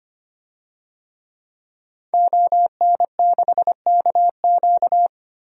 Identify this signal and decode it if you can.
Morse code, heard as a keyed tone, about 25 wpm, 700 Hz. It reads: ON6KQ
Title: DXCC-100 Call Signs - 5 Characters - 25wpm